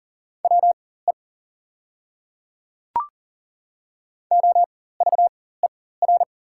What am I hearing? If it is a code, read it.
Morse code, heard as a keyed tone, about 38 wpm, spaced as for 18 wpm, 700 Hz. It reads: WE OVER
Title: 4x Character Spacing - Top 200 Words - 40wpm